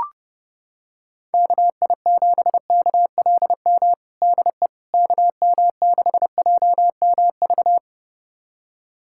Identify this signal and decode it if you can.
Morse code, heard as a keyed tone, about 30 wpm, 700 Hz. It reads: KI7KLM DE KM6JMV
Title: USA Callsign DE Callsign - 6 Characters - 30wpm